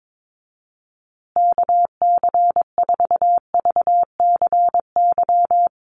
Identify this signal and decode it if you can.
Morse code, heard as a keyed tone, about 22 wpm, 700 Hz. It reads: KC4VCY
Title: USA Call Signs Repeated 2x 22wpm